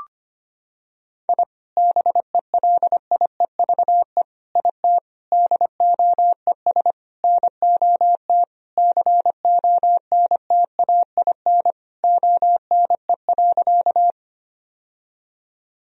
Morse code, heard as a keyed tone, about 25 wpm, 700 Hz. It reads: I BELIEVE IT DOES NOT CONTAIN ONE.
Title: Sentences from Top 1000 Words - Review Part 2 - 25wpm